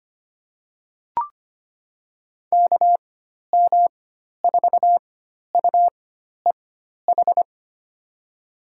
Morse code, heard as a keyed tone, about 25 wpm, spaced as for 12 wpm, 700 Hz. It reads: KM4UEH
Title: USA Call Signs - 6 Characters - 4x Character Spacing - 25wpm